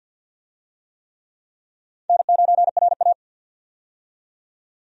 Morse code, heard as a keyed tone, about 52 wpm, 700 Hz. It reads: N9RA
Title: USA Call Signs - 4 Characters - 50wpm